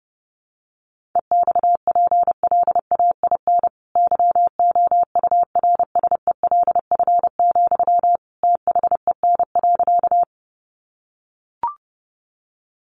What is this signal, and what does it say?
Morse code, heard as a keyed tone, about 30 wpm, 700 Hz. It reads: EXPLAIN YOURSELF, THEN.
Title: Sentences from Top 1100 Words 30wpm